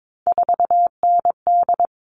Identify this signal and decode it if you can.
Morse code, heard as a keyed tone, about 22 wpm, 700 Hz. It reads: 4ND
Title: All Cut Numbers - 3-Digit 22wpm